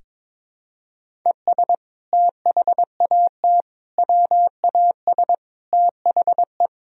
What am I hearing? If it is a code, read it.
Morse code, heard as a keyed tone, about 21 wpm, 700 Hz. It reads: ES THAT WAS THE
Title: Sentences from Top 200 Words and Abbreviations 22wpm